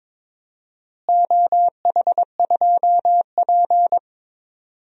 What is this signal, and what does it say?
Morse code, heard as a keyed tone, about 21 wpm, 700 Hz. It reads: OH2P
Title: DXCC-100 Call Signs - 4 Characters - Repeated 3x - 22wpm